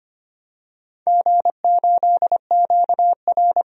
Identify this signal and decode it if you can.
Morse code, heard as a keyed tone, about 25 wpm, 700 Hz. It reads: G8QR
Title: DXCC-100 Call Signs - 4 Characters - 25wpm